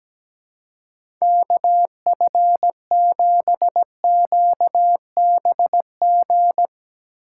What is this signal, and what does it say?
Morse code, heard as a keyed tone, about 17 wpm, 700 Hz. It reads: KF7QBG
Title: USA Call Signs - 3x Character Spacing - 17wpm